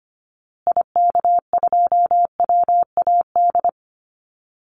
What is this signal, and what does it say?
Morse code, heard as a keyed tone, about 25 wpm, 700 Hz. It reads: IK2WAD
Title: Call Signs Repeated 2x - 25wpm